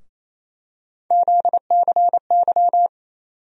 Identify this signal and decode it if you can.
Morse code, heard as a keyed tone, about 28 wpm, 700 Hz. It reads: ZCY
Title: USA Call Sign Postfixes - 3 Letters 28wpm